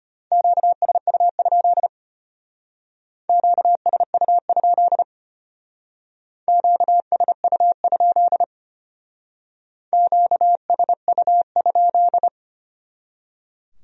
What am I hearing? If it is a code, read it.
Morse code, heard as a keyed tone, about 33 wpm, 700 Hz. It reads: QSU? QSU? QSU? QSU?